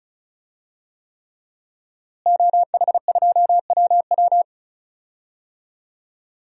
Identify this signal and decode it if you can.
Morse code, heard as a keyed tone, about 34 wpm, 700 Hz. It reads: OH2WW
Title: DXCC-100 Call Signs - 5 Characters - Repeated 3x - 35wpm